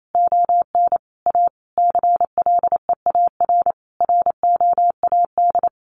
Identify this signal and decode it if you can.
Morse code, heard as a keyed tone, about 28 wpm, 700 Hz. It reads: ON A CLEAR ROAD
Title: Sentences from Top 800 Words 28wpm